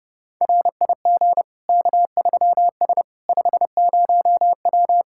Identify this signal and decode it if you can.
Morse code, heard as a keyed tone, about 32 wpm, 700 Hz. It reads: RIG K3S 50W
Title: Practice QSO - Exchange 2 - Info - 30wpm